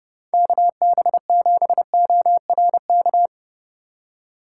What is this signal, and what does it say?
Morse code, heard as a keyed tone, about 37 wpm, 700 Hz. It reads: KB7ORK
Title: Speed Racing - USA 2x3 Call Signs 30wpm